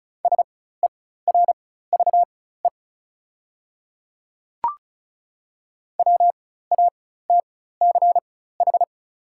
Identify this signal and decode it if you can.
Morse code, heard as a keyed tone, about 36 wpm, spaced as for 16 wpm, 700 Hz. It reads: SERVE WATCH
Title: 4x Character Spacing - Top 500 Words - 35wpm